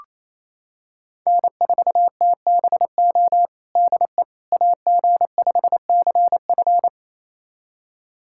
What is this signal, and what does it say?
Morse code, heard as a keyed tone, about 28 wpm, 700 Hz. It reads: N4TBO DE AG5CF
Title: USA Callsign DE Callsign - 5 Characters - 28wpm